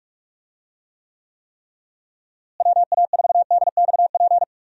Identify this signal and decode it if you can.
Morse code, heard as a keyed tone, about 46 wpm, 700 Hz. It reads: WA4DXP
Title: USA Call Signs - 6 Characters - Repeated 3x - 45wpm